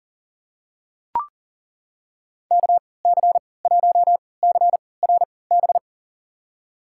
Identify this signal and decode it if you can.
Morse code, heard as a keyed tone, about 37 wpm, spaced as for 22 wpm, 700 Hz. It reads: KC1CRB
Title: USA Call Signs - 6 Characters - 3x Character Spacing - 40wpm